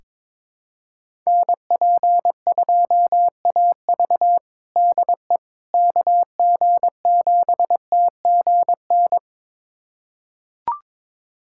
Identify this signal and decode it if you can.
Morse code, heard as a keyed tone, about 22 wpm, 700 Hz. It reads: NP2AV DE KG7TGN
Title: USA Callsign DE Callsign 22wpm